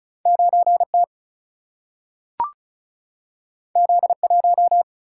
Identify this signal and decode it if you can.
Morse code, heard as a keyed tone, about 36 wpm, 700 Hz. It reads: 9T Z1
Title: Single-Letter Single-Number 35wpm